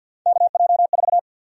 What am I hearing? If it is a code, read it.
Morse code, heard as a keyed tone, about 48 wpm, 700 Hz. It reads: KJ4
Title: USA Call Sign Prefixes - 50wpm